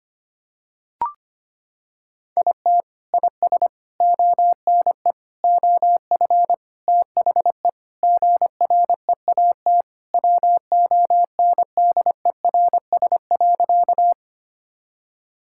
Morse code, heard as a keyed tone, about 25 wpm, 700 Hz. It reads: IT IS ONE OF THE GREAT WONDERS.
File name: practice / 25wpm / Sentences with 4-Word Phrases - 25wpm